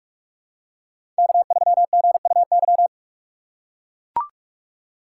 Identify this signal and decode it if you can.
Morse code, heard as a keyed tone, about 43 wpm, 700 Hz. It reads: K3GUY
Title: USA Call Signs - 5 Characters - 4x Character Spacing - 45wpm